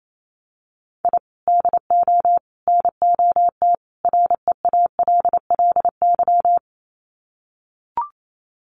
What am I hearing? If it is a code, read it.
Morse code, heard as a keyed tone, about 28 wpm, 700 Hz. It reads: I DO NOT REALLY
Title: Sets of 4 Words 28wpm